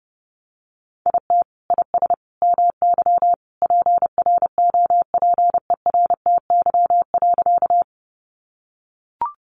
Morse code, heard as a keyed tone, about 30 wpm, 700 Hz. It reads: IT IS MY PROPERTY.
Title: Sentences from Top 1000 Words 30wpm